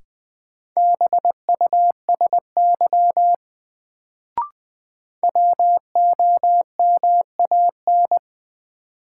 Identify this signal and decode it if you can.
Morse code, heard as a keyed tone, about 21 wpm, 700 Hz. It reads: BUSY WOMAN
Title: Sentences from Top 1000 Words - Review Part 2 - New Word Drill - 20wpm